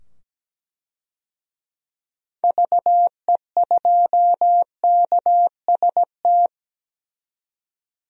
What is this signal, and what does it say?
Morse code, heard as a keyed tone, about 17 wpm, 700 Hz. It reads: VE2KST
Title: DXCC-100 Call Signs - 6 Characters - 17wpm